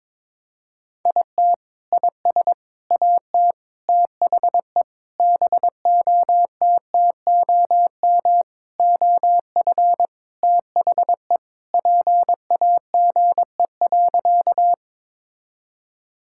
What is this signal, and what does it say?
Morse code, heard as a keyed tone, about 22 wpm, 700 Hz. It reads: IT IS AT THE BOTTOM OF THE PAGE.
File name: practice / 22wpm / Sentences with 4-Word Phrases - 22wpm